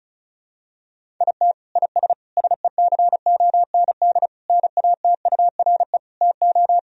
Morse code, heard as a keyed tone, about 34 wpm, 700 Hz. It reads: IT IS SECOND NATURE TO
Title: Sentences from Top 1000 Words - Review Part 1 - 35wpm